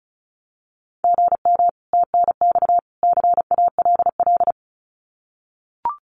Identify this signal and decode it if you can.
Morse code, heard as a keyed tone, about 34 wpm, 700 Hz. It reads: GM TNX CALL
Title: Common QSO Phrases 35wpm